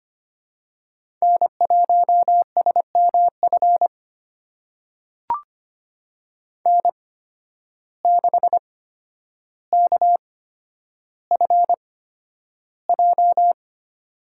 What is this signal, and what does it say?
Morse code, heard as a keyed tone, about 25 wpm, spaced as for 11 wpm, 700 Hz. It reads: N1SMF N6KFJ
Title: USA Call Signs - 5 Characters - 8x Character Spacing - 25wpm